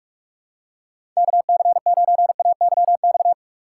Morse code, heard as a keyed tone, about 46 wpm, 700 Hz. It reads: KC9AYX
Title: USA Call Signs - 6 Characters - 4x Character Spacing - 45wpm